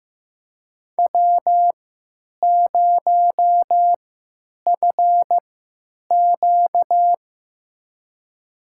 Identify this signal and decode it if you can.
Morse code, heard as a keyed tone, about 15 wpm, spaced as for 9 wpm, 700 Hz. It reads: W0FQ
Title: USA Call Signs - 4 Characters - 3x Character Spacing - 15wpm